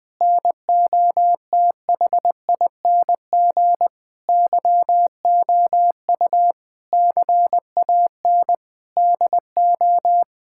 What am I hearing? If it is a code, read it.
Morse code, heard as a keyed tone, about 20 wpm, 700 Hz. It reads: NOTHING YOU CAN DO
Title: Sets of 8 Words - 20wpm